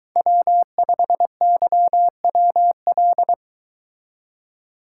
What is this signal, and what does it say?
Morse code, heard as a keyed tone, about 22 wpm, 700 Hz. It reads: W5YWL